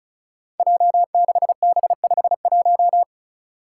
Morse code, heard as a keyed tone, about 36 wpm, 700 Hz. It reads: J6B51
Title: Five Challenging Letter-Number Combinations 35wpm